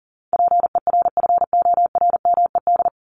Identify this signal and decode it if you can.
Morse code, heard as a keyed tone, about 40 wpm, 700 Hz. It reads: PERFORMED
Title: ED Suffix - Top 100 Words with Base Words 40wpm